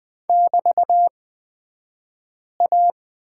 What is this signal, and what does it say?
Morse code, heard as a keyed tone, about 20 wpm, spaced as for 5 wpm, 700 Hz. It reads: <BT>A
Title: 223 Lesson 36d - BT - Two Characters 20wpm